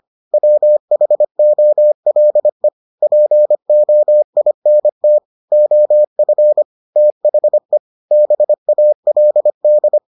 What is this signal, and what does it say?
Morse code, heard as a keyed tone, about 25 wpm, 600 Hz. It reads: WHOLE POINT OF THE BALD